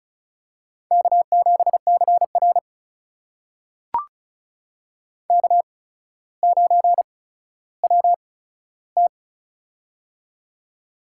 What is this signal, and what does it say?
Morse code, heard as a keyed tone, about 36 wpm, spaced as for 15 wpm, 700 Hz. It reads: K7CR K9WT
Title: USA Call Signs - 4 Characters - 8x Character Spacing - 35wpm